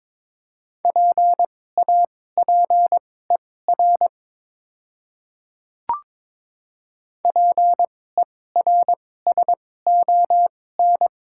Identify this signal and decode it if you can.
Morse code, heard as a keyed tone, about 22 wpm, spaced as for 16 wpm, 700 Hz. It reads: PAPER PERSON
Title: 2x Character Spacing - Top 500 Words - 22wpm